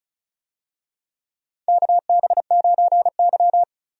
Morse code, heard as a keyed tone, about 34 wpm, 700 Hz. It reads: KB9Y
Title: USA Call Signs - 4 Characters - 35wpm